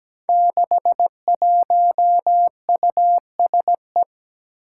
Morse code, heard as a keyed tone, about 17 wpm, 700 Hz. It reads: 61USE
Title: Five Challenging Letter-Number Combinations 17wpm